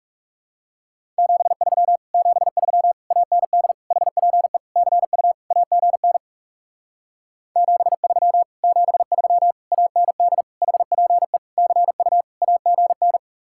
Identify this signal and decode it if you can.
Morse code, heard as a keyed tone, about 38 wpm, 700 Hz. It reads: 73 73 AND HPE CU AGN 73 73 AND HPE CU AGN